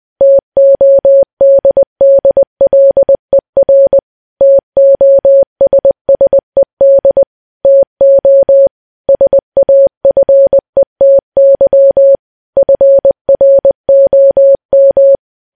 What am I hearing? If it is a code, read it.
Morse code, heard as a keyed tone, about 20 wpm, 550 Hz. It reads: TODDLER TOSSED TO SAFETY FROM
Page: news-headlines-in-morse-code-at-20-wpm